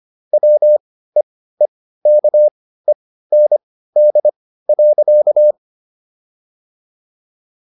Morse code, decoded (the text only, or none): WEEKEND.